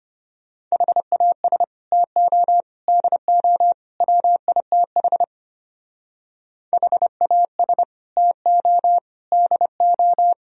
Morse code, heard as a keyed tone, about 30 wpm, 700 Hz. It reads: HAS TO DO WITH HAS TO DO
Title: Speed-Racing - Sets of 4 Words 25wpm